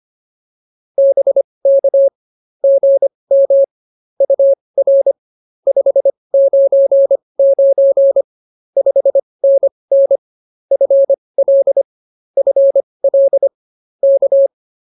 Morse code, heard as a keyed tone, about 24 wpm, spaced as for 20 wpm, 550 Hz. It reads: BK GM UR 599 5NN FL FL K